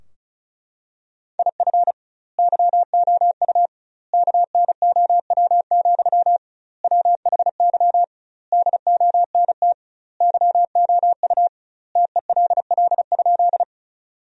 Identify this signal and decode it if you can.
Morse code, heard as a keyed tone, about 34 wpm, 700 Hz. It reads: IF YOU KNOW, WHY DONT YOU TELL?
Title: 2x Word-Spacing - Sentences from Top 300 Words - 35wpm